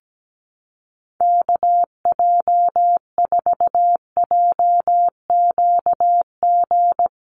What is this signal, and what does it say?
Morse code, heard as a keyed tone, about 17 wpm, 700 Hz. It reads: KJ4JQG
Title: USA Call Signs Repeated 3x 17wpm